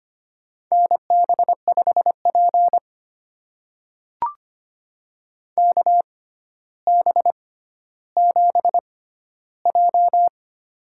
Morse code, heard as a keyed tone, about 25 wpm, spaced as for 13 wpm, 700 Hz. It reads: NB5P KB7J
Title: USA Call Signs - 4 Characters - 6x Character Spacing - 25wpm